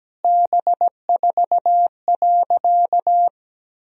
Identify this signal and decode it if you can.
Morse code, heard as a keyed tone, about 17 wpm, 700 Hz. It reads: B4.
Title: Sentences from Top 400 Words and Abbreviations 17wpm